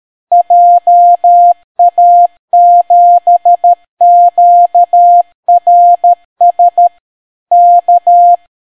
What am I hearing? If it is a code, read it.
Morse code, heard as a keyed tone, about 13 wpm, 700 Hz. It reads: JA7QRS K